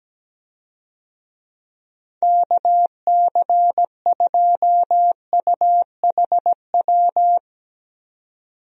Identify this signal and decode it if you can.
Morse code, heard as a keyed tone, about 17 wpm, 700 Hz. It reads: KC2UHW
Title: USA Call Signs - 6 Characters - 17wpm